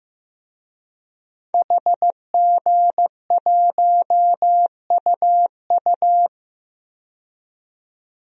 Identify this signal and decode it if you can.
Morse code, heard as a keyed tone, about 15 wpm, 700 Hz. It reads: HG1UU
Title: DXCC-100 Call Signs - 5 Characters - 15wpm